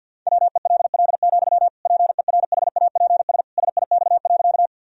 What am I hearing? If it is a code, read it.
Morse code, heard as a keyed tone, about 51 wpm, 700 Hz. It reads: WELL, PERHAPS SIX.
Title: Sentences from Top 1000 Words - Review Part 2 - 50wpm